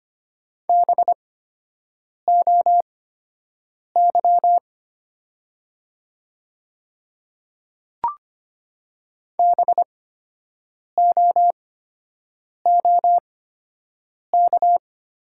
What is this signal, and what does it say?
Morse code, heard as a keyed tone, about 25 wpm, spaced as for 7 wpm, 700 Hz. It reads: BOY BOOK
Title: 8x Character Spacing - Top 300 Words - 25wpm